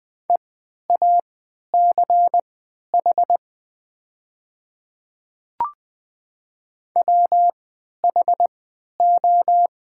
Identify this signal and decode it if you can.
Morse code, heard as a keyed tone, about 20 wpm, spaced as for 11 wpm, 700 Hz. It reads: EACH WHO
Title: 3x Character Spacing - Top 100 Words - 20wpm